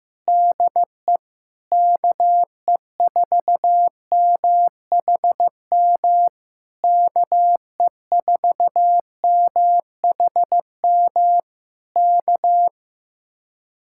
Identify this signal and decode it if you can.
Morse code, heard as a keyed tone, about 15 wpm, 700 Hz. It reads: DE KE4MHM KE4MHM K
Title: Speed-Racing Common QSO Phrases Encore - 15wpm